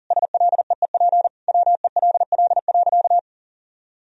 Morse code, heard as a keyed tone, about 37 wpm, 700 Hz. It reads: SLEEP WELL.
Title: Sentences from Top 500 Words and Abbreviations 40wpm